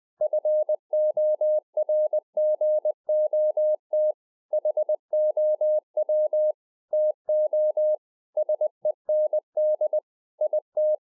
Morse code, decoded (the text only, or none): FORGOT HOW TO SEND IT